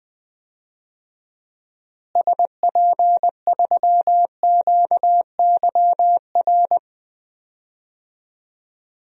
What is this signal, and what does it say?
Morse code, heard as a keyed tone, about 20 wpm, 700 Hz. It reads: SP3QYR
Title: DXCC-100 Call Signs - 6 Characters - Repeated 3x - 20wpm